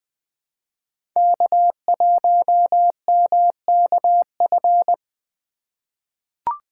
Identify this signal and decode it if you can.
Morse code, heard as a keyed tone, about 20 wpm, 700 Hz. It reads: K1MKF
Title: USA Call Signs - 5 Characters - 2x Character Spacing - 20wpm